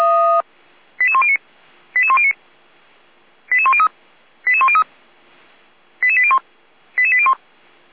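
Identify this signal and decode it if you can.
Five-tone selective call (ZVEI), heard as quick runs of five tones, tones of about 70 ms. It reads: 88109 88109 88103 88103 88801 88801
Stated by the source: ZVEI five-tone selective call received on SDR.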